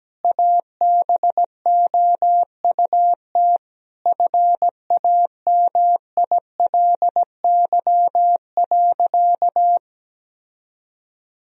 Morse code, heard as a keyed tone, about 17 wpm, 700 Hz. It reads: ABOUT FAMILY.
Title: Sentences from Top 1000 Words - Review Part 3 - 17wpm